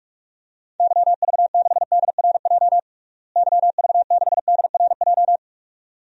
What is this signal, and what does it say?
Morse code, heard as a keyed tone, about 46 wpm, 700 Hz. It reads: YV6DRJ YV6DRJ
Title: DXCC-100 Call Signs - 6 Characters - Repeated 2x - 45wpm